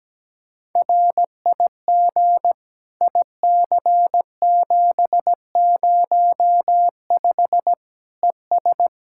Morse code, heard as a keyed tone, about 17 wpm, 700 Hz. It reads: RIG IC705 ES
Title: Practice QSO - Exchange 2 - Info - 17wpm